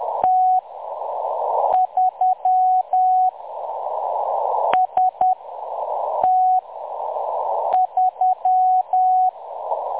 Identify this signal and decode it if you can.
Morse code, heard as a keyed tone, about 10 wpm, spaced as for 6 wpm, 740 Hz. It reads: T3 ST3
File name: M23-ST3- -0858utc-10184khz